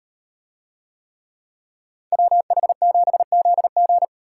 Morse code, heard as a keyed tone, about 38 wpm, 700 Hz. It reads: WH7ZG